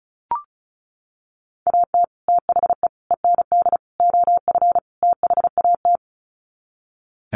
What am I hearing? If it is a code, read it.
Morse code, heard as a keyed tone, about 34 wpm, 700 Hz. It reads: AT THE END OF THAT
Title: Sets of 5 Words Encore2 35wpm